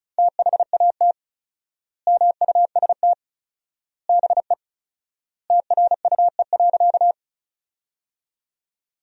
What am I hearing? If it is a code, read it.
Morse code, heard as a keyed tone, about 36 wpm, 700 Hz. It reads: THAT MUST BE TRUE.
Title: 4x Word-Spacing - Sentences from Top 400 Words - 35wpm